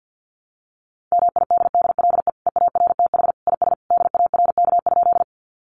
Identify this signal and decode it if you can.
Morse code, heard as a keyed tone, about 50 wpm, 700 Hz. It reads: MIDDLE EARTH IS DARK?
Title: Sentences from Top 900 Words 50wpm